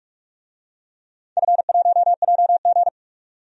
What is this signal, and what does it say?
Morse code, heard as a keyed tone, about 46 wpm, 700 Hz. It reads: F1JG
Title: DXCC-100 Call Signs - 4 Characters - 45wpm